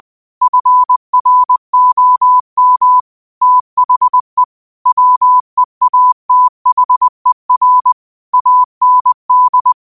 Morse code, heard as a keyed tone, about 20 wpm, 1000 Hz. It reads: FROM THE WEATHER AND
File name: qotdfiles / 20wpm